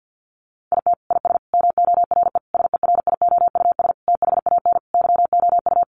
Morse code, heard as a keyed tone, about 50 wpm, 700 Hz. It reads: IT IS MORE SERIOUS THAN YOU
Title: Sentences from Top 1100 Words 50wpm